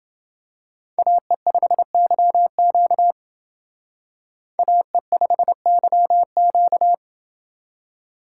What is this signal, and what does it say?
Morse code, heard as a keyed tone, about 28 wpm, 700 Hz. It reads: AE5YQ AE5YQ